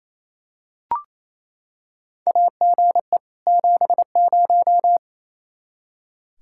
Morse code, heard as a keyed tone, about 28 wpm, 700 Hz. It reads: AGE 70
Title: Practice QSO - Exchange 3 - More Info - 28wpm